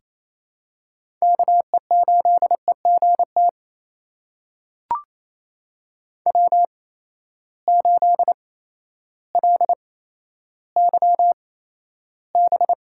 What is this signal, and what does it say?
Morse code, heard as a keyed tone, about 28 wpm, spaced as for 13 wpm, 700 Hz. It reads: KE8EGT W8LYB
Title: USA Call Signs - 8x Character Spacing - 28wpm